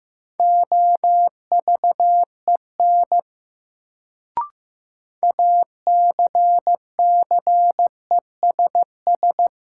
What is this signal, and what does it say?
Morse code, heard as a keyed tone, about 15 wpm, 700 Hz. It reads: OVEN ACCESS
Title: Sentences from Top 1000 Words - Review Part 4 - New Word Drill - 15wpm